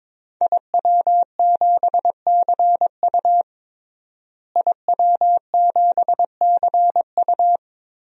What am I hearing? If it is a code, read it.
Morse code, heard as a keyed tone, about 21 wpm, 700 Hz. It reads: IW7CU IW7CU
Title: DXCC-100 Call Signs - 5 Characters - Repeated 2x - 22wpm